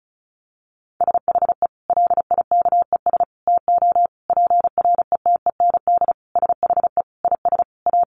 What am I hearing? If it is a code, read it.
Morse code, heard as a keyed tone, about 34 wpm, 700 Hz. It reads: SHE LIKES TO PRETEND SHE IS A
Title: Additional Easy Phrases 35wpm